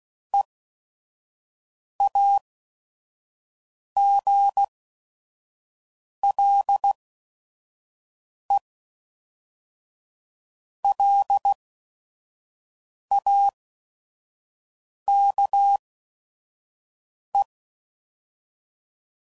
Morse code, heard as a keyed tone, about 16 wpm, spaced as for 5 wpm, 800 Hz. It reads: EAGLELAKE
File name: CW Code Groups @5WPM / fromSM's / Cities